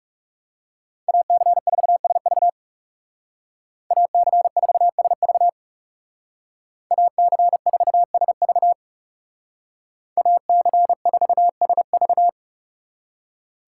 Morse code, heard as a keyed tone, about 37 wpm, 700 Hz. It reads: AC4SV AC4SV AC4SV AC4SV
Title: Speed Racing - USA 2x2 Call Signs 30wpm